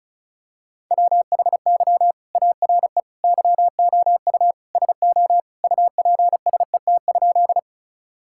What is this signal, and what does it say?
Morse code, heard as a keyed tone, about 36 wpm, 700 Hz. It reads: WHY ARE YOU SO UPSET?
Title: Sentences from Top 1000 Words - Review Part 5 - 35wpm